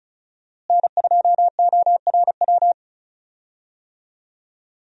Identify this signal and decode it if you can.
Morse code, heard as a keyed tone, about 36 wpm, 700 Hz. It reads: N2ORW
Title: USA Call Signs - 5 Characters - Repeated 3x - 35wpm